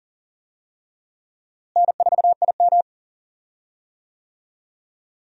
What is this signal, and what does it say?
Morse code, heard as a keyed tone, about 40 wpm, 700 Hz. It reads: N4IM
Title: USA Call Signs - 4 Characters - Repeated 2x - 40wpm